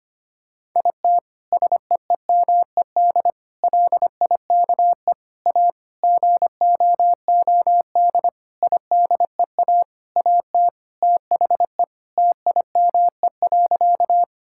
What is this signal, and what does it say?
Morse code, heard as a keyed tone, about 25 wpm, 700 Hz. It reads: IT SEEMED LIKE A GOOD IDEA AT THE TIME.
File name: practice / 25wpm / Sentences with 9-Word Phrases - 25wpm